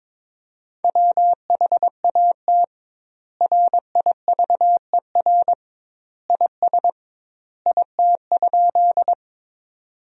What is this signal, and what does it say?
Morse code, heard as a keyed tone, about 22 wpm, 700 Hz. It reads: WHAT RIVER IS IT?
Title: 2x Word-Spacing - Sentences from Top 300 Words - 22wpm